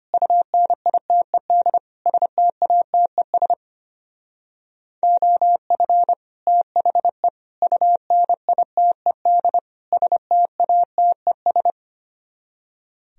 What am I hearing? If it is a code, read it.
Morse code, heard as a keyed tone, about 33 wpm, 700 Hz. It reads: UNITED STATES OF THE UNITED STATES